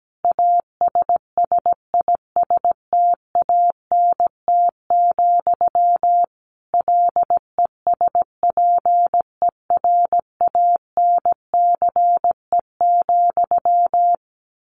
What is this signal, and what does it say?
Morse code, heard as a keyed tone, about 17 wpm, 700 Hz. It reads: ASSISTANT, LESPERANCE,
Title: A Sound Of Thunder 17wpm